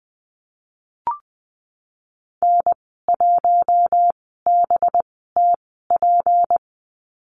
Morse code, heard as a keyed tone, about 20 wpm, spaced as for 15 wpm, 700 Hz. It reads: N1BTP
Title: 231 Lesson 37g - Letter x - Call Signs 20wpm